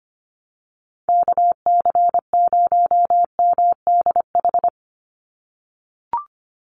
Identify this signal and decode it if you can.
Morse code, heard as a keyed tone, about 25 wpm, 700 Hz. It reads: KC0MDH
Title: USA Call Signs 25wpm